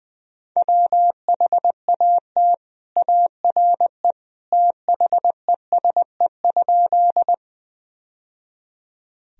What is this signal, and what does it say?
Morse code, heard as a keyed tone, about 20 wpm, 700 Hz. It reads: WHAT ARE THESE?
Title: Sentences from Top 1000 Words - Review Part 1 - 20wpm